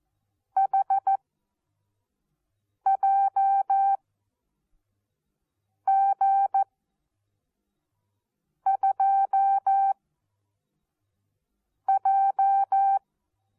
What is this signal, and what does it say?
Morse code, heard as a keyed tone, about 14 wpm, spaced as for 4 wpm, 780 Hz. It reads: HJG2J
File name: Onlinelehrgang / morsen / DJ4UF Morsen14